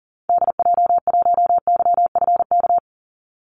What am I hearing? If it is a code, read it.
Morse code, heard as a keyed tone, about 40 wpm, 700 Hz. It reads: DJ1YFK
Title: Call Signs Repeated 2x - 40wpm